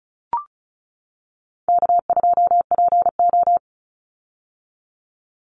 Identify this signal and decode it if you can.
Morse code, heard as a keyed tone, about 36 wpm, 700 Hz. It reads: K2PO
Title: Call Signs - 35wpm